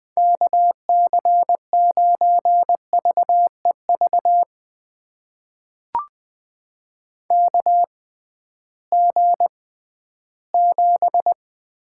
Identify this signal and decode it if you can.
Morse code, heard as a keyed tone, about 20 wpm, spaced as for 13 wpm, 700 Hz. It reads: KC9VEV KG7
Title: USA Call Signs - 6 Characters - 6x Character Spacing - 20wpm